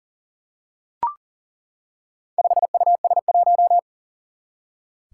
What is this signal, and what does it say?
Morse code, heard as a keyed tone, about 41 wpm, 700 Hz. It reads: 5US1
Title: Four Challenging Letter-Number Combinations 40wpm